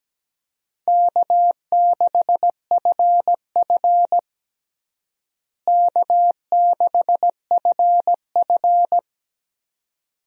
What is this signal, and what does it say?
Morse code, heard as a keyed tone, about 17 wpm, 700 Hz. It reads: K6FF K6FF
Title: USA Call Signs - 4 Characters - Repeated 2x - 17wpm